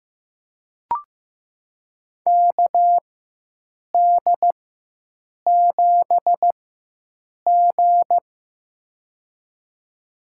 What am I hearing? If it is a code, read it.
Morse code, heard as a keyed tone, about 15 wpm, spaced as for 7 wpm, 700 Hz. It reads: KD7G
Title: USA Call Signs - 4 Characters - 4x Character Spacing - 15wpm